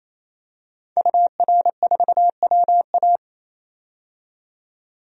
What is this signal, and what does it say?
Morse code, heard as a keyed tone, about 28 wpm, 700 Hz. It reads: UR4WA
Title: DXCC-100 Call Signs - 5 Characters - 28wpm